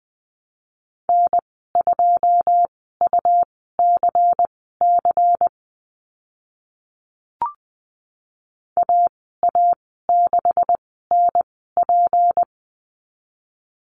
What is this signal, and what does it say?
Morse code, heard as a keyed tone, about 20 wpm, spaced as for 14 wpm, 700 Hz. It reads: N2UCC AA6NP
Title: 154 Lesson 25g - Letter p - Call Signs 20wpm